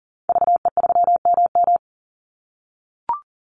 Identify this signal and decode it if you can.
Morse code, heard as a keyed tone, about 40 wpm, 700 Hz. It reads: VE3MM
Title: Call Signs - 40wpm